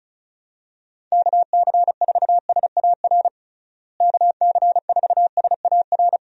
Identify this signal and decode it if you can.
Morse code, heard as a keyed tone, about 34 wpm, 700 Hz. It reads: KC4SAR KC4SAR
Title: USA Call Signs - 6 Characters - Repeated 2x - 35wpm